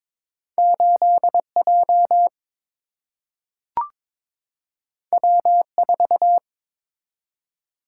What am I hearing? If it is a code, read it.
Morse code, heard as a keyed tone, about 22 wpm, 700 Hz. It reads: 8 J W 4
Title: Single-Letter Single-Number 22wpm